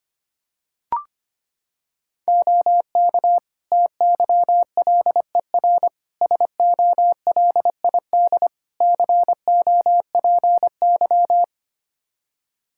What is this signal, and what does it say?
Morse code, heard as a keyed tone, about 25 wpm, 700 Hz. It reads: OK TYLER SOLID COPY
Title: Practice QSO - Exchange 3 - More Info - 25wpm